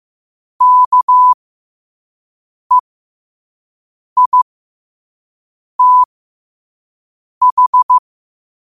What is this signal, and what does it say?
Morse code, heard as a keyed tone, about 15 wpm, spaced as for 5 wpm, 1000 Hz. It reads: KEITH
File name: qotdfiles / 5wpm